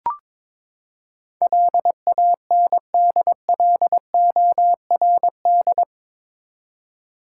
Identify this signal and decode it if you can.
Morse code, heard as a keyed tone, about 22 wpm, 700 Hz. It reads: LANDLORD
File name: practice / 22wpm / Eight Letter Words - 22wpm